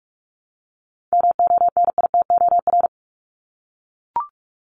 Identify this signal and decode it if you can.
Morse code, heard as a keyed tone, about 44 wpm, 700 Hz. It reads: MONITOR
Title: Top 1751-2000 Words 45wpm